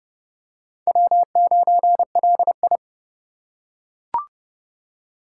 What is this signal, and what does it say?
Morse code, heard as a keyed tone, about 28 wpm, 700 Hz. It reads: W9LI